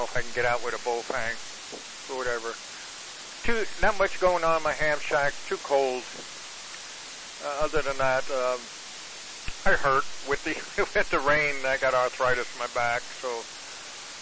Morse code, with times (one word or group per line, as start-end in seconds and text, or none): none